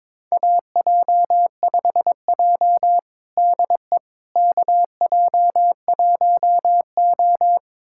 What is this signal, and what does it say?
Morse code, heard as a keyed tone, about 22 wpm, 700 Hz. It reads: AJ5J DE KJ1O
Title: USA Callsign DE Callsign - 4 Characters - 22wpm